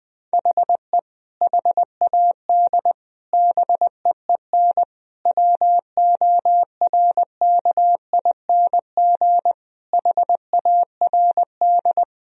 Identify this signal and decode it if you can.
Morse code, heard as a keyed tone, about 20 wpm, 700 Hz. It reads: HE HAD BEEN WORKING HARD
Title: Sentences from Top 1000 Words - Review Part 4 - 20wpm